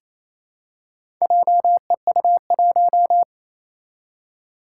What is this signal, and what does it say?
Morse code, heard as a keyed tone, about 27 wpm, 700 Hz. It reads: JEU1
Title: Four Challenging Letter-Number Combinations 28wpm